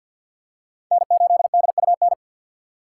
Morse code, heard as a keyed tone, about 52 wpm, 700 Hz. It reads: N8DUN
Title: USA Call Signs - 5 Characters - 2x Character Spacing - 50wpm